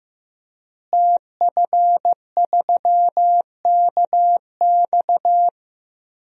Morse code, heard as a keyed tone, about 15 wpm, 700 Hz. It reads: TF3KX
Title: DXCC-100 Call Signs - 5 Characters - Repeated 2x - 15wpm